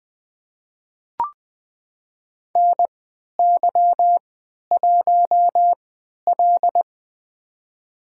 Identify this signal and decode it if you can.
Morse code, heard as a keyed tone, about 20 wpm, spaced as for 11 wpm, 700 Hz. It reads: NY1L
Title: USA Call Signs - 4 Characters - 3x Character Spacing - 20wpm